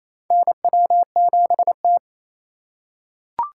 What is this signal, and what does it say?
Morse code, heard as a keyed tone, about 28 wpm, 700 Hz. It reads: NW7T
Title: USA Call Signs - 4 Characters - 4x Character Spacing - 28wpm